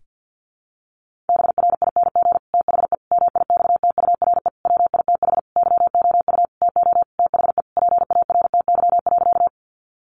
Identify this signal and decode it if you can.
Morse code, heard as a keyed tone, about 50 wpm, 700 Hz. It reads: BRING THE MIXTURE WITH YOU TO THE PARTY.
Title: Sentences from Top 1500 Words 50wpm